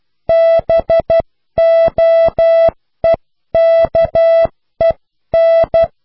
Morse code, heard as a keyed tone, about 12 wpm, 660 Hz. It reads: BOEKEN